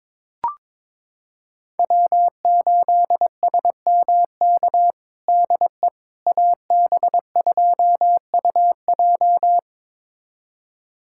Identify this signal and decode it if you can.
Morse code, heard as a keyed tone, about 22 wpm, 700 Hz. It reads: W8SMK DE AB2UJ
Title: USA Callsign DE Callsign - 5 Characters - 22wpm